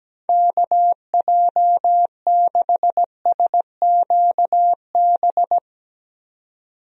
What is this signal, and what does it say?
Morse code, heard as a keyed tone, about 17 wpm, 700 Hz. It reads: KJ6SQB
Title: USA Call Signs - 6 Characters - 17wpm